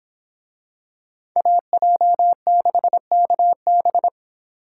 Morse code, heard as a keyed tone, about 26 wpm, 700 Hz. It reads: AJ6KB